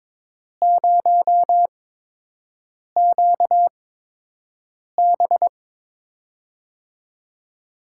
Morse code, heard as a keyed tone, about 22 wpm, spaced as for 6 wpm, 700 Hz. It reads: 0QB
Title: USA Call Signs - 5 Characters - 8x Character Spacing - 22wpm